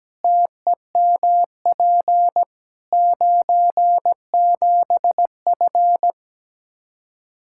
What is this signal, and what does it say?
Morse code, heard as a keyed tone, about 17 wpm, 700 Hz. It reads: TEMP 97F
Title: Practice QSO - Exchange 2 - Info - 17wpm